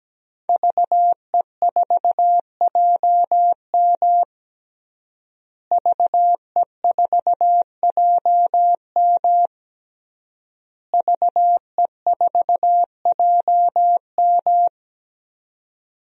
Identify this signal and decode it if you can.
Morse code, heard as a keyed tone, about 17 wpm, 700 Hz. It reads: VE4JM VE4JM VE4JM
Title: DXCC-100 Call Signs - 5 Characters - Repeated 3x - 17wpm